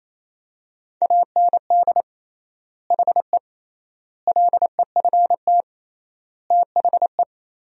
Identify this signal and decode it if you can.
Morse code, heard as a keyed tone, about 28 wpm, 700 Hz. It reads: AND HE LEFT THE
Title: 3x Word-Spacing - Sentences from Top 300 Words - 28wpm